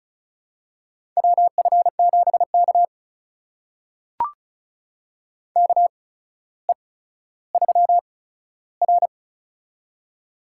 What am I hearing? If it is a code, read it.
Morse code, heard as a keyed tone, about 36 wpm, spaced as for 15 wpm, 700 Hz. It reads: WF7K KE3R
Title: USA Call Signs - 4 Characters - 8x Character Spacing - 35wpm